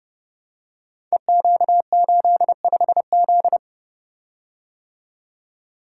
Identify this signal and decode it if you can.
Morse code, heard as a keyed tone, about 30 wpm, 700 Hz. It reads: EQ85Z
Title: Code Groups - 5 Letter:Numbers - 30wpm